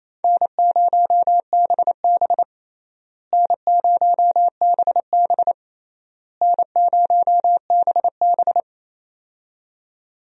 Morse code, heard as a keyed tone, about 28 wpm, 700 Hz. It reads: N0BB N0BB N0BB
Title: USA Call Signs - 4 Characters - Repeated 3x - 28wpm